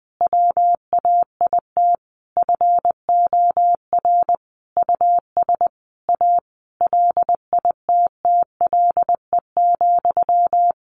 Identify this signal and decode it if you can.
Morse code, heard as a keyed tone, about 20 wpm, 700 Hz. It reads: WAIT FOR US A LITTLE,
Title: Sentences from Top 900 Words 20wpm